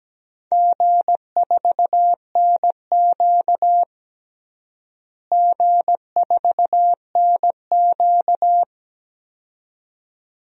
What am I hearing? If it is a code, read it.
Morse code, heard as a keyed tone, about 17 wpm, 700 Hz. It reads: G4NQ G4NQ
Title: DXCC-100 Call Signs - 4 Characters - Repeated 2x - 17wpm